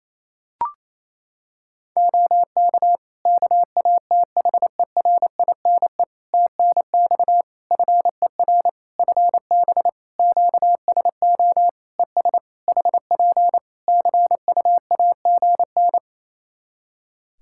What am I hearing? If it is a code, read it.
Morse code, heard as a keyed tone, about 27 wpm, 700 Hz. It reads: OK KATHERINE TNX FER FB QSO ES HP CUAGN
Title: K1USN SST QSO Practice 28wpm